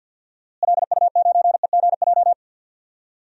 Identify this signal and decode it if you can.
Morse code, heard as a keyed tone, about 49 wpm, 700 Hz. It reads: LU9EGJ
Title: DXCC-100 Call Signs - 6 Characters - Repeated 2x - 50wpm